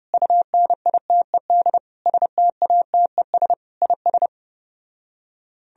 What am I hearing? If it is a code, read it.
Morse code, heard as a keyed tone, about 34 wpm, 700 Hz. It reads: UNITED STATES IS